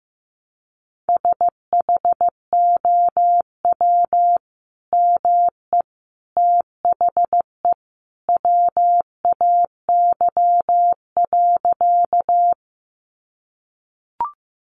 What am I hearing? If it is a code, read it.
Morse code, heard as a keyed tone, about 15 wpm, 700 Hz. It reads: SHOW ME THE WAY.
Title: Sentences from Top 200 Words 15wpm